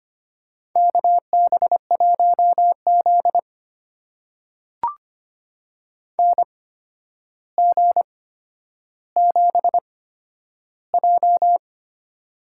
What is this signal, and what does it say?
Morse code, heard as a keyed tone, about 25 wpm, spaced as for 11 wpm, 700 Hz. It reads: KB1Z NG7J
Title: USA Call Signs - 8x Character Spacing - 25wpm